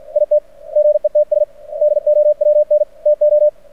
Morse code, heard as a keyed tone, about 51 wpm, 600 Hz. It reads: IT GETS DOWN TO
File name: archived / CRSnet